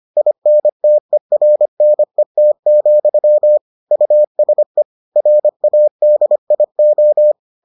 Morse code, heard as a keyed tone, about 25 wpm, 600 Hz. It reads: INTERNET, USE RADIO